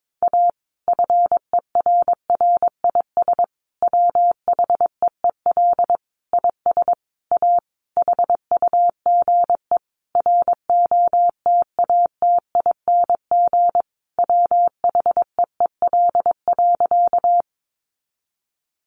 Morse code, heard as a keyed tone, about 22 wpm, 700 Hz. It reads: A FERRIS WHEEL IS A HUGE ROTATING WHEEL.
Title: Additional Easy Phrases 22wpm